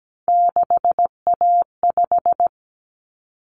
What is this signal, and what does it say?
Morse code, heard as a keyed tone, about 17 wpm, 700 Hz. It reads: 6A5
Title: Common Cut Numbers - 3-Digit 17wpm